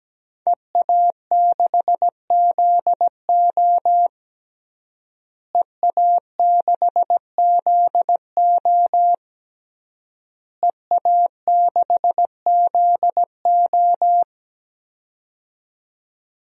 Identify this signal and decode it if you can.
Morse code, heard as a keyed tone, about 17 wpm, 700 Hz. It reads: EA6ZO EA6ZO EA6ZO
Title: DXCC-100 Call Signs - 5 Characters - Repeated 3x - 17wpm